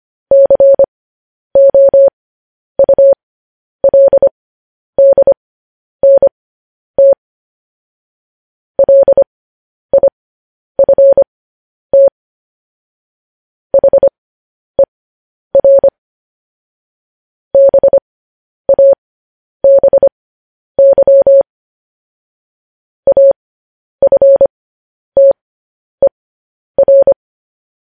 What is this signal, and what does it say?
Morse code, heard as a keyed tone, about 25 wpm, spaced as for 10 wpm, 550 Hz. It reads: COULDNT LIFT HER BABY AFTER
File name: podcast / News headlines-1x-25x10